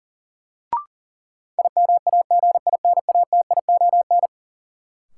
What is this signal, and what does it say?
Morse code, heard as a keyed tone, about 40 wpm, 700 Hz. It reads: IMAGINATION
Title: Sentences from Top 1300 Words - New Word Drill - 40wpm